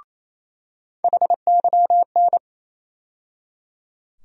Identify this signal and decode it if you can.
Morse code, heard as a keyed tone, about 28 wpm, 700 Hz. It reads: HYN
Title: USA Call Sign Postfixes - 3 Letters 28wpm